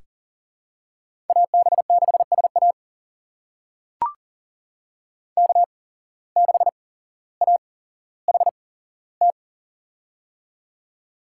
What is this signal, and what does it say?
Morse code, heard as a keyed tone, about 40 wpm, spaced as for 17 wpm, 700 Hz. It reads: AB6SA K6AHT
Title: USA Call Signs - 5 Characters - 8x Character Spacing - 40wpm